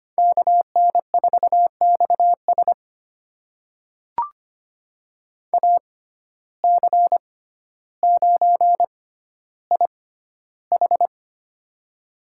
Morse code, heard as a keyed tone, about 25 wpm, spaced as for 13 wpm, 700 Hz. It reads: KN4XS AC9IH
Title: USA Call Signs - 5 Characters - 6x Character Spacing - 25wpm